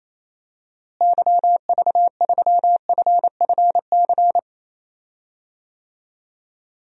Morse code, heard as a keyed tone, about 27 wpm, 700 Hz. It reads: YV3FFC
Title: DXCC-100 Call Signs - 6 Characters - Repeated 3x - 28wpm